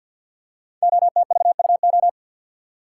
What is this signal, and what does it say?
Morse code, heard as a keyed tone, about 53 wpm, 700 Hz. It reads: OTVUO
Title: Code Groups - 5 Letter:Numbers - 50wpm